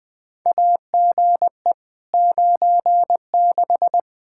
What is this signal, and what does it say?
Morse code, heard as a keyed tone, about 21 wpm, 700 Hz. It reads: AGE 96
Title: Practice QSO - Exchange 3 - More Info - 20wpm